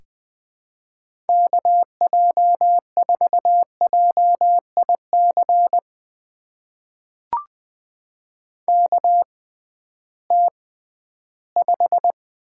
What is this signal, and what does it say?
Morse code, heard as a keyed tone, about 20 wpm, spaced as for 13 wpm, 700 Hz. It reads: KJ4JIC KT5
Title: USA Call Signs - 6x Character Spacing - 20wpm